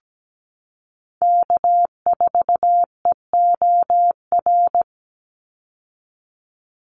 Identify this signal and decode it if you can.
Morse code, heard as a keyed tone, about 17 wpm, 700 Hz. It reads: K4EOR
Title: Call Signs - 17wpm